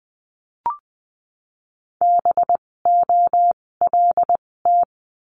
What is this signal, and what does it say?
Morse code, heard as a keyed tone, about 20 wpm, spaced as for 16 wpm, 700 Hz. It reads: BOLT
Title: 256 Lesson 42a - Words 12-20wpm